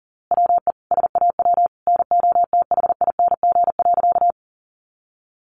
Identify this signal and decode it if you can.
Morse code, heard as a keyed tone, about 40 wpm, 700 Hz. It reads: WE SAW NOTHING.
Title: A Sound Of Thunder 40wpm